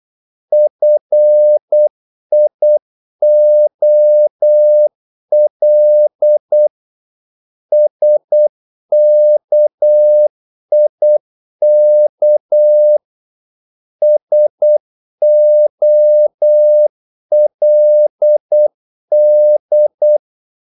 Morse code, heard as a keyed tone, about 8 wpm, 600 Hz. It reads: FIOL SKIK SOLD
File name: Korte ord DK 8wpm